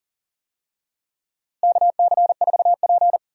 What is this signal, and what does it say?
Morse code, heard as a keyed tone, about 39 wpm, 700 Hz. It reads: KC4P
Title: USA Call Signs - 4 Characters - Repeated 2x - 40wpm